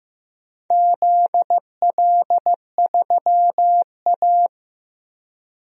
Morse code, heard as a keyed tone, about 15 wpm, 700 Hz. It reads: ZL3A
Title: DXCC-100 Call Signs - 4 Characters - 15wpm